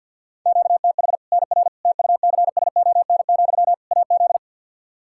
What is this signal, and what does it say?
Morse code, heard as a keyed tone, about 51 wpm, 700 Hz. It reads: QTH NR TUCSON, AZ
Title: K1USN SST QSO Practice 50wpm